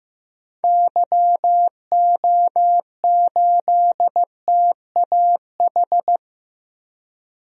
Morse code, heard as a keyed tone, about 16 wpm, 700 Hz. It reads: YO8TAH
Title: DXCC-100 Call Signs - 6 Characters - Repeated 2x - 15wpm